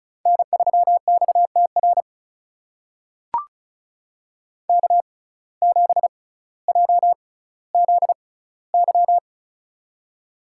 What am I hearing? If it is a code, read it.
Morse code, heard as a keyed tone, about 36 wpm, spaced as for 18 wpm, 700 Hz. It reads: N3XTR K7JZY
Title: USA Call Signs - 5 Characters - 6x Character Spacing - 35wpm